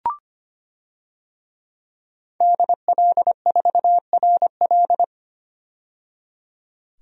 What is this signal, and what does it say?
Morse code, heard as a keyed tone, about 25 wpm, 700 Hz. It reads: DL4RL
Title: DXCC-100 Call Signs - 5 Characters - 25wpm